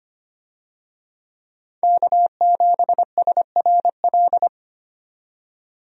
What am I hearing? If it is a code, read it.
Morse code, heard as a keyed tone, about 25 wpm, 700 Hz. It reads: K7SRL
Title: USA Call Signs - 5 Characters - 25wpm